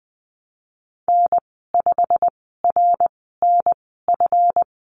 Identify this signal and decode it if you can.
Morse code, heard as a keyed tone, about 20 wpm, spaced as for 15 wpm, 700 Hz. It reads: N5RNF
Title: 161 Lesson 26g - Letter g - Call Signs 20wpm